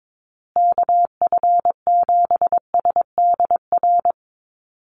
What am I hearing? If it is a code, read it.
Morse code, heard as a keyed tone, about 22 wpm, 700 Hz. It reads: KF7SDR
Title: USA Call Signs Repeated 2x 22wpm